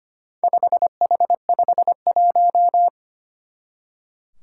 Morse code, heard as a keyed tone, about 25 wpm, 700 Hz. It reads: 5H51
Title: Four Challenging Letter-Number Combinations 25wpm